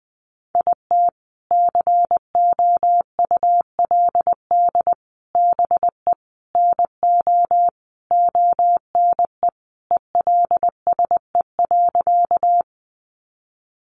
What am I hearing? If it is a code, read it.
Morse code, heard as a keyed tone, about 20 wpm, 700 Hz. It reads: IT COULD BE NO ONE ELSE.
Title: Sentences from Top 800 Words 20wpm